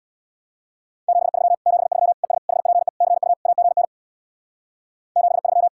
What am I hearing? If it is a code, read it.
Morse code, heard as a keyed tone, about 64 wpm, 700 Hz. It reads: 73 73 ES HPE CU AGN 73